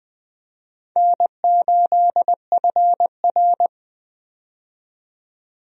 Morse code, heard as a keyed tone, about 20 wpm, 700 Hz. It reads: N8FR
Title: USA Call Signs - 4 Characters - 20wpm